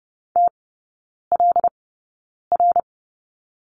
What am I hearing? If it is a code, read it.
Morse code, heard as a keyed tone, about 30 wpm, spaced as for 9 wpm, 700 Hz. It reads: TLR
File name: course / 30:10wpm / 087 Lesson 15e - Number 5 - Three Characters 30wpm